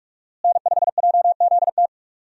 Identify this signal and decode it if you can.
Morse code, heard as a keyed tone, about 46 wpm, 700 Hz. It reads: N5JZT
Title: USA Call Signs - 5 Characters - Repeated 2x - 45wpm